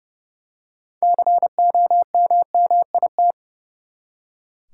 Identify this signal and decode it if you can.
Morse code, heard as a keyed tone, about 29 wpm, 700 Hz. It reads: COMMIT
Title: Six Letter Words - 30wpm